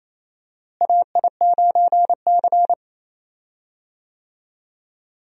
Morse code, heard as a keyed tone, about 28 wpm, 700 Hz. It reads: AI9C
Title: USA Call Signs - 4 Characters - Repeated 3x - 28wpm